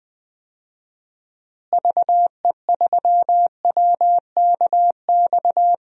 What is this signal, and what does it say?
Morse code, heard as a keyed tone, about 21 wpm, 700 Hz. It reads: VE3WKX
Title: DXCC-100 Call Signs - 6 Characters - Repeated 3x - 20wpm